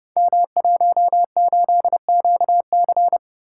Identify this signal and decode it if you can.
Morse code, heard as a keyed tone, about 31 wpm, 700 Hz. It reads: M18QC
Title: Code Groups - 5 Letter:Numbers - 30wpm